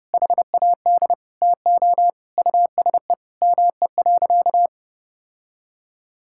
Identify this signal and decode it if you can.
Morse code, heard as a keyed tone, about 30 wpm, 700 Hz. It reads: HAD TO USE ME.
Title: Sentences from Top 1000 Words - Review Part 6 - 30wpm